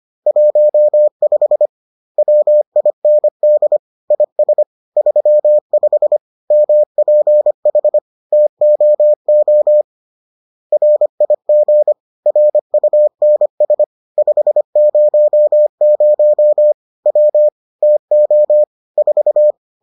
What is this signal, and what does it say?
Morse code, heard as a keyed tone, about 25 wpm, 600 Hz. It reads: 15 WIND IS 35 MPH TOO RIG RUNS 500 W TO 4